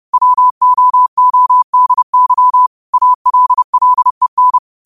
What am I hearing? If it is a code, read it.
Morse code, heard as a keyed tone, about 30 wpm, 1000 Hz. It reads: WOODY ALLEN